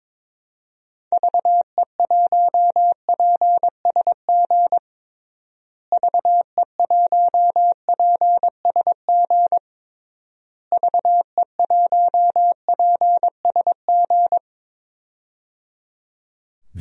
Morse code, heard as a keyed tone, about 22 wpm, 700 Hz. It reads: VE1PSG VE1PSG VE1PSG
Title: DXCC-100 Call Signs - 6 Characters - Repeated 3x - 22wpm